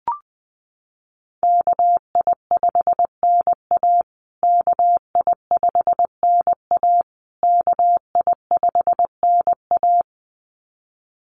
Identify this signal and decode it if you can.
Morse code, heard as a keyed tone, about 20 wpm, 700 Hz. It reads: KI5NA KI5NA KI5NA
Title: USA Call Signs Repeated 3x 20wpm